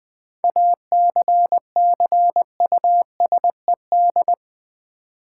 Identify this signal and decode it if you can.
Morse code, heard as a keyed tone, about 20 wpm, 700 Hz. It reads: ACCUSED
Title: 7 Letter Words - First 3 Letters Given - 20wpm